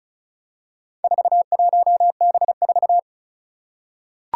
Morse code, heard as a keyed tone, about 36 wpm, 690 Hz. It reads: 41B4
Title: Four Challenging Letter-Number Combinations 35wpm